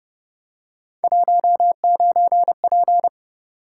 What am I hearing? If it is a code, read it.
Morse code, heard as a keyed tone, about 31 wpm, 700 Hz. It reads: 19P